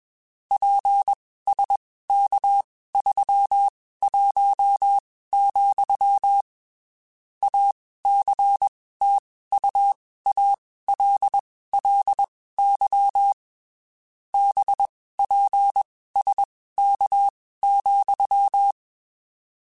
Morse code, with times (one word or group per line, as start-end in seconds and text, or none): 0.5-6.4 PSK31,
7.4-13.3 ACTUALLY
14.3-18.7 BPSK,